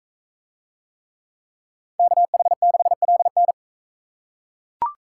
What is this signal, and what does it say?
Morse code, heard as a keyed tone, about 41 wpm, 700 Hz. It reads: KH6LN